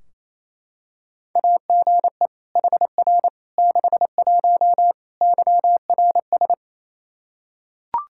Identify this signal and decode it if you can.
Morse code, heard as a keyed tone, about 28 wpm, 700 Hz. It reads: AGE HR 61 YRS
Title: Practice QSO - Exchange 3 - More Info - 28wpm